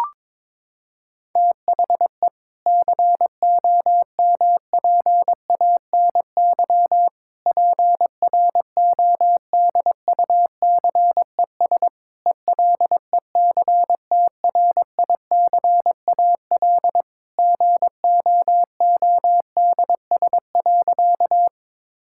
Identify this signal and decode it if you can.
Morse code, heard as a keyed tone, about 22 wpm, 700 Hz. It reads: THE COMPANY PRODUCES ELECTRICAL GOODS.
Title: Sentences from Top 1000 Words - Review Part 6 - 22wpm